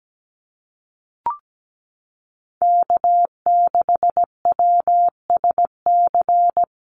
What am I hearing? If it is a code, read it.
Morse code, heard as a keyed tone, about 17 wpm, 700 Hz. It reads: K6WSC
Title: USA Call Signs Repeated 2x 17wpm